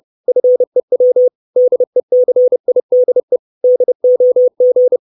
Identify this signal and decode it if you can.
Morse code, heard as a keyed tone, about 30 wpm, 500 Hz. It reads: FEW DECIDE DOG
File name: 632 words / 632 Words 30WPM